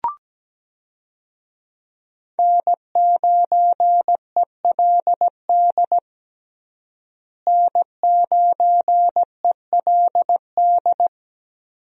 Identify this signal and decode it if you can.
Morse code, heard as a keyed tone, about 17 wpm, 700 Hz. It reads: N9ELD N9ELD
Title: USA Call Signs - 5 Characters - Repeated 2x - 17wpm